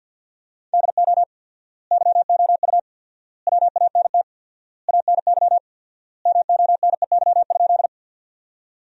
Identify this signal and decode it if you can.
Morse code, heard as a keyed tone, about 50 wpm, 700 Hz. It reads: DO YOU WANT ANY MONEY?
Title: 4x Word-Spacing - Sentences from Top 400 Words - 50wpm